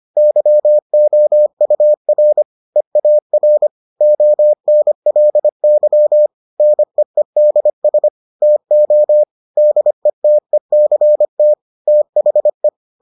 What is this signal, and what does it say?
Morse code, heard as a keyed tone, about 25 wpm, 600 Hz. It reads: YOUR EAR ONLY NEEDS TO DETECT THE